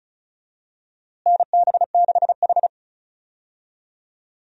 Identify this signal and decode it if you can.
Morse code, heard as a keyed tone, about 36 wpm, 700 Hz. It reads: NB6H
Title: USA Call Signs - 4 Characters - Repeated 3x - 35wpm